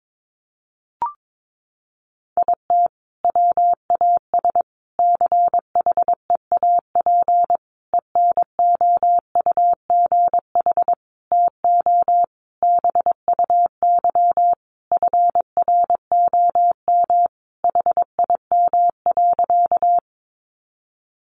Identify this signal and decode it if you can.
Morse code, heard as a keyed tone, about 22 wpm, 700 Hz. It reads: IT WAS CHEAP ENOUGH TO BUY FROM HIM.
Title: Sentences from Top 1200 Words 22wpm